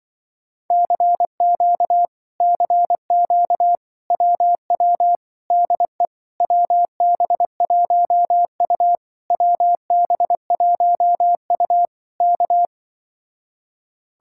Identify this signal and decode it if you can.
Morse code, heard as a keyed tone, about 24 wpm, 700 Hz. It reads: CQ CQ WW DE WB1U WB1U K
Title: Speed-Racing Common QSO Phrases - 20wpm